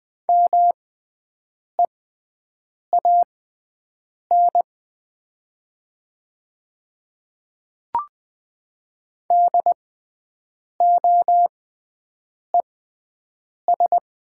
Morse code, heard as a keyed tone, about 20 wpm, spaced as for 7 wpm, 700 Hz. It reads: MEAN DOES
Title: 6x Character Spacing - Top 200 Words - 20wpm